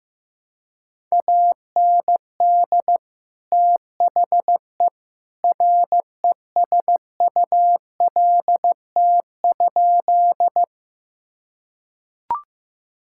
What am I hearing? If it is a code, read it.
Morse code, heard as a keyed tone, about 15 wpm, 700 Hz. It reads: AND THE RESULT?
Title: Sentences from Top 1000 Words - Review Part 5 - 15wpm